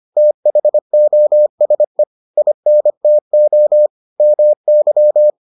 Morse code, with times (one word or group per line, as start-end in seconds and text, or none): 0.2-2.0 THOSE
2.4-3.9 INTO
4.2-5.3 MY